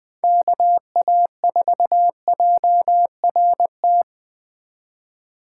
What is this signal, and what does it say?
Morse code, heard as a keyed tone, about 20 wpm, 700 Hz. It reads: KA4JRT